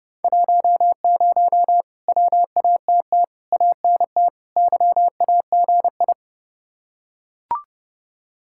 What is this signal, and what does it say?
Morse code, heard as a keyed tone, about 28 wpm, 700 Hz. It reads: 10 WATT ANT YAGI